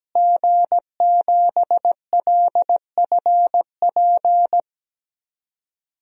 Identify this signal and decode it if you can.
Morse code, heard as a keyed tone, about 17 wpm, 700 Hz. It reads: G7LFP
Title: DXCC-100 Call Signs - 5 Characters - Repeated 3x - 17wpm